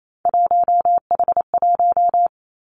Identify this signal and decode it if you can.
Morse code, heard as a keyed tone, about 28 wpm, 700 Hz. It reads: 1H1
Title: Three Challenging Letter-Number Combinations 28wpm